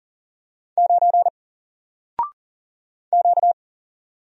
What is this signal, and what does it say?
Morse code, heard as a keyed tone, about 40 wpm, 700 Hz. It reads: 9 Q
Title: Single Letter-Number 40wpm